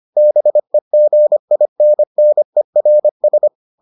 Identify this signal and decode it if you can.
Morse code, heard as a keyed tone, about 25 wpm, 600 Hz. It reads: BEGINNERS